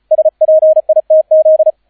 Morse code, heard as a keyed tone, about 34 wpm, 630 Hz, with radio noise. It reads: SPITZ